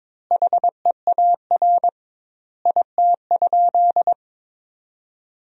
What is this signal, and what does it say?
Morse code, heard as a keyed tone, about 22 wpm, 700 Hz. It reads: HEAR IT?
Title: 2x Word-Spacing - Sentences from Top 300 Words - 22wpm